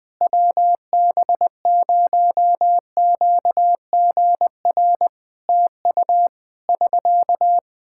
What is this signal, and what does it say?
Morse code, heard as a keyed tone, about 20 wpm, 700 Hz. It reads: WB0QGR TU <SK>
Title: Practice QSO - Exchange 4 - Ending - 20wpm